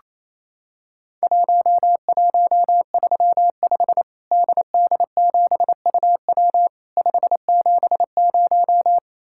Morse code, heard as a keyed tone, about 28 wpm, 700 Hz. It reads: 1135 DD7UW 570
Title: Worked All Europe DX Contest - QTC Exchange - 28wpm